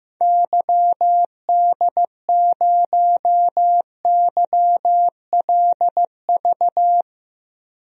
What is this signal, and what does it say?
Morse code, heard as a keyed tone, about 15 wpm, 700 Hz. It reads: YD0YLV
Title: DXCC-100 Call Signs - 6 Characters - 15wpm